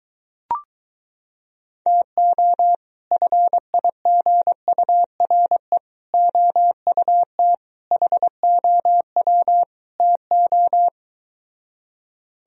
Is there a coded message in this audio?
Yes: Morse code from 1.9 to 10.9 s, heard as a keyed tone, about 23 wpm, 700 Hz. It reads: TO FIGURE OUT HOW TO